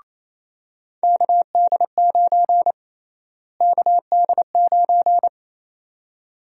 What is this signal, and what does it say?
Morse code, heard as a keyed tone, about 28 wpm, 700 Hz. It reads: KD9 KD9
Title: USA Call Sign Prefixes - Repeated 2x - 28wpm